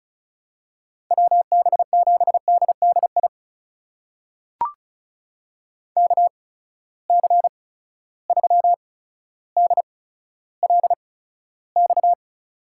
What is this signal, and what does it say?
Morse code, heard as a keyed tone, about 36 wpm, spaced as for 15 wpm, 700 Hz. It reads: WB7DDI KC3DLX
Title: USA Call Signs - 6 Characters - 8x Character Spacing - 35wpm